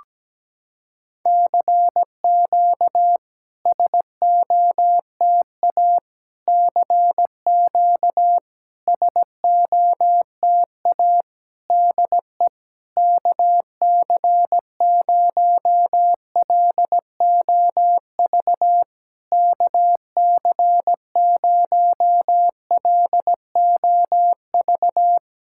Morse code, heard as a keyed tone, about 17 wpm, 700 Hz. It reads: CQ SOTA CQ SOTA DE KC0LOV KC0LOV
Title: SOTA QSO Practice 17wpm